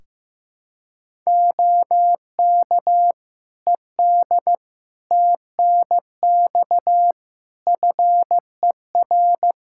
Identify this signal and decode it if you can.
Morse code, heard as a keyed tone, about 15 wpm, 700 Hz. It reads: OK ED TNX FER
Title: K1USN SST QSO Practice 15wpm